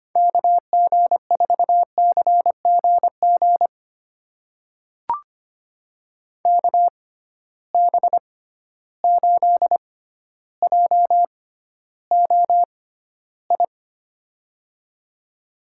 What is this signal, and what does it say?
Morse code, heard as a keyed tone, about 25 wpm, spaced as for 13 wpm, 700 Hz. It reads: KG4CGG KB8JOI
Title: USA Call Signs - 6x Character Spacing - 25wpm